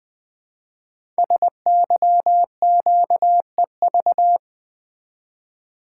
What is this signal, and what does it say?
Morse code, heard as a keyed tone, about 20 wpm, 700 Hz. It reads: SYQEV
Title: Code Groups - 5 Letters - 20wpm